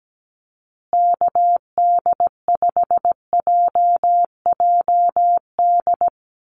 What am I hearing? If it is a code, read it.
Morse code, heard as a keyed tone, about 17 wpm, 700 Hz. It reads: KD5JJD
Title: USA Call Signs Repeated 3x 17wpm